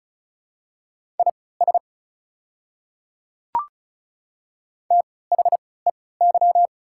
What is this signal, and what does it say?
Morse code, heard as a keyed tone, about 36 wpm, spaced as for 20 wpm, 690 Hz. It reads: IS THEY
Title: 3x Character Spacing - Top 500 Words - 35wpm